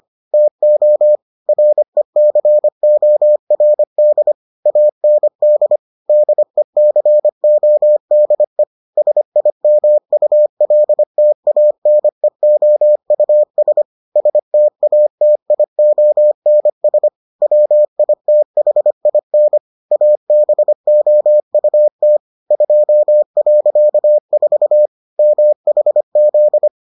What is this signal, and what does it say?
Morse code, heard as a keyed tone, about 25 wpm, 600 Hz. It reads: TO RECORD AND DECODE SIMULTANEOUS STATIONS WITHIN ABOUT 2.4 MHZ